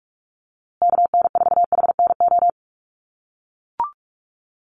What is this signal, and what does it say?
Morse code, heard as a keyed tone, about 44 wpm, 700 Hz. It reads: KN4HNO
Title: USA Call Signs Repeated 3x 45wpm